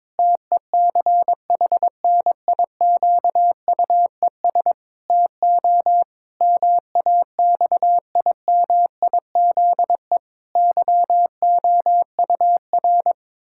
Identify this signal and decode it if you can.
Morse code, heard as a keyed tone, about 22 wpm, 700 Hz. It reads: TECHNIQUES TO MAXIMIZE YOUR
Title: Pangram Challenge v2 - 22wpm